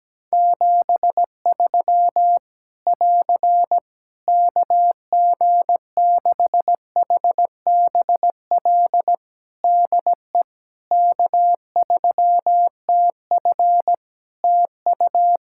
Morse code, heard as a keyed tone, about 17 wpm, 700 Hz. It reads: 73 <AR> KG6HBL DE K3TF TU
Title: Practice QSO - Exchange 4 - Ending - 17wpm